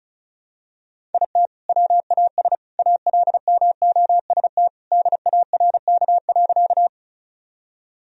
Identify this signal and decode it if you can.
Morse code, heard as a keyed tone, about 34 wpm, 700 Hz. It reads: IT WAS ALMOST DARK.
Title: Sentences from Top 1000 Words - Review Part 4 - 35wpm